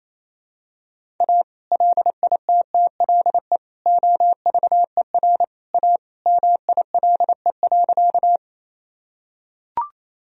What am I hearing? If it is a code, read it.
Morse code, heard as a keyed tone, about 28 wpm, 700 Hz. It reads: A LITTLE OVER A MILE.
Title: Sentences from Top 1000 Words - Review Part 1 - 28wpm